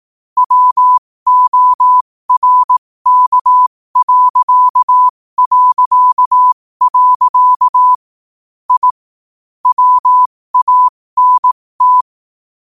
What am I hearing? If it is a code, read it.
Morse code, heard as a keyed tone, about 18 wpm, spaced as for 15 wpm, 1000 Hz. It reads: WORK... I WANT
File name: qotdfiles / 15wpm